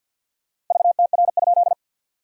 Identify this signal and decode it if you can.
Morse code, heard as a keyed tone, about 50 wpm, 690 Hz. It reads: VTR?
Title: SOTA QSO Practice 50wpm